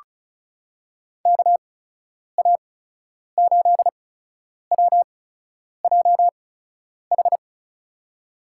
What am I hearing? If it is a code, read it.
Morse code, heard as a keyed tone, about 36 wpm, spaced as for 10 wpm, 700 Hz. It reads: KA8WJH
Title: USA Call Signs - 6 Characters - 8x Character Spacing - 35wpm